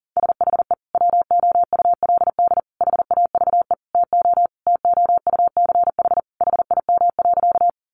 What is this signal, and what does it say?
Morse code, heard as a keyed tone, about 40 wpm, 700 Hz. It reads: SHE WOULD HAVE TO TOUCH HIM.
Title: Sentences from Top 1200 Words 40wpm